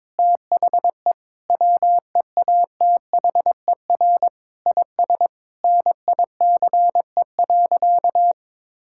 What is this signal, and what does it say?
Morse code, heard as a keyed tone, about 22 wpm, 700 Hz. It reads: THE WEATHER IS NICE.
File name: Difficult Sentences with Hints - 22wpm